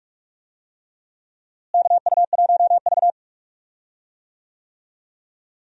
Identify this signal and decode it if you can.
Morse code, heard as a keyed tone, about 42 wpm, 700 Hz. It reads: KU1V
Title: USA Call Signs - 4 Characters - Repeated 3x - 45wpm